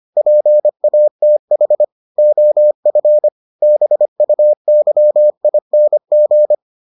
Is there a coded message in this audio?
Yes: Morse code from 0.2 to 6.5 s, heard as a keyed tone, about 25 wpm, 600 Hz. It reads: PATH OF BUYING